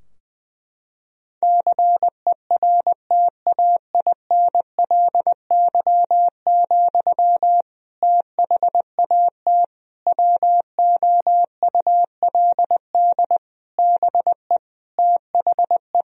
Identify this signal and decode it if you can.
Morse code, heard as a keyed tone, about 20 wpm, 700 Hz. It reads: CERTAINLY, THAT WOULD BE THE
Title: Sentences from Top 1000 Words - Review Part 2 - 20wpm